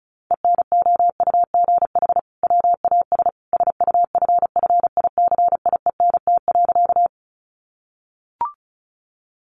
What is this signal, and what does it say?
Morse code, heard as a keyed tone, about 34 wpm, 700 Hz. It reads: ENOUGH WAS SUFFICIENT.
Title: Sentences from Top 1400 Words 35wpm